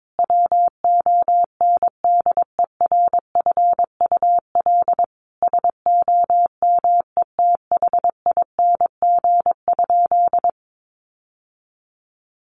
Morse code, heard as a keyed tone, about 22 wpm, 700 Hz. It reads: WONDERFUL SOMETHING?
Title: Sentences from Top 1100 Words 22wpm